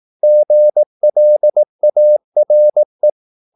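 Morse code, heard as a keyed tone, about 18 wpm, 600 Hz. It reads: GLARE
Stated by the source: Morse practice at 18 wpm, English text, from CW med Gnister.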